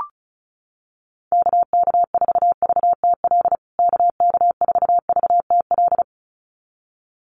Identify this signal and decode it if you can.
Morse code, heard as a keyed tone, about 34 wpm, 700 Hz. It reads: KK4VTL KK4VTL
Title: USA Call Signs Repeated 2x 35wpm